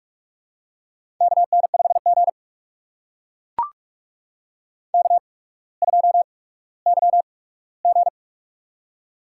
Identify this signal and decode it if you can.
Morse code, heard as a keyed tone, about 44 wpm, spaced as for 19 wpm, 700 Hz. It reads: KN5G K2YG
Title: USA Call Signs - 4 Characters - 8x Character Spacing - 45wpm